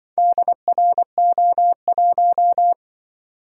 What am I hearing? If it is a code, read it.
Morse code, heard as a keyed tone, about 24 wpm, 700 Hz. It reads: DRO1
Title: Speed-Racing - Four Letter-Number Combinations - 20wpm